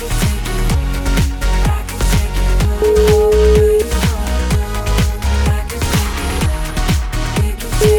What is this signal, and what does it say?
Ringback tone (Europe, 425 Hz), with noise.